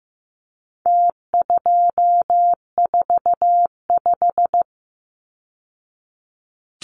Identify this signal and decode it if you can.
Morse code, heard as a keyed tone, about 15 wpm, 700 Hz. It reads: T245
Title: Numbers - Contest Style - 4-Digit 15wpm